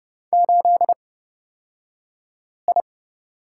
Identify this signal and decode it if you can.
Morse code, heard as a keyed tone, about 30 wpm, spaced as for 5 wpm, 700 Hz. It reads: 8I
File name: Lesson 38d - Letter q - Two Characters 30wpm